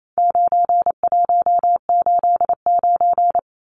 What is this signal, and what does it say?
Morse code, heard as a keyed tone, about 28 wpm, 700 Hz. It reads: 9189
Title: Numbers - 4-Digit 28wpm